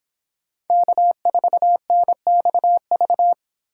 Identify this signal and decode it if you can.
Morse code, heard as a keyed tone, about 26 wpm, 700 Hz. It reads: K4NXV